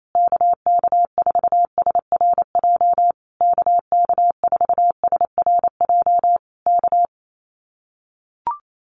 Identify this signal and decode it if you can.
Morse code, heard as a keyed tone, about 28 wpm, 700 Hz. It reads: KK4SRJ KK4SRJ K
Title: Common QSO Phrases Encore 28wpm